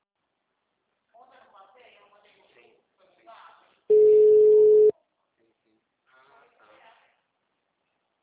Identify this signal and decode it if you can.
Ringback tone (Europe, 425 Hz), with speech.
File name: out- -2057- - - .2776